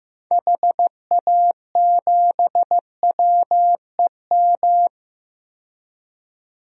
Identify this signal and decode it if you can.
Morse code, heard as a keyed tone, about 15 wpm, 690 Hz. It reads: HA7WEM
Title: DXCC-100 Call Signs - 6 Characters - 15wpm